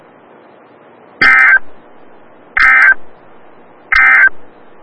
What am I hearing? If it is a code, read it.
Emergency Alert System (SAME) tones, heard as harsh buzzing bursts. It is heard from 1.1 to 4.4 s.
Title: Des Moines, IA NOAA Weather Radio - WXL57